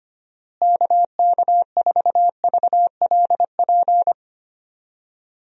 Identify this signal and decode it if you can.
Morse code, heard as a keyed tone, about 25 wpm, 700 Hz. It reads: KK4VLP
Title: USA Call Signs - 6 Characters - Repeated 3x - 25wpm